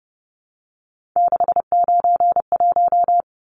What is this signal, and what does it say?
Morse code, heard as a keyed tone, about 30 wpm, 700 Hz. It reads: 691
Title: Numbers - 3-Digit 30wpm